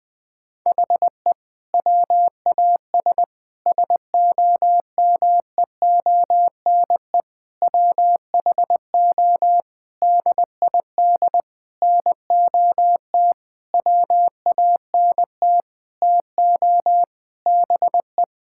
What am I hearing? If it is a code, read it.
Morse code, heard as a keyed tone, about 20 wpm, 700 Hz. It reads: HE WAS SOMEONE WHO DID NOT WANT TO BE
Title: Sentences with 6-Word Phrases - Spoken Beginning - 20wpm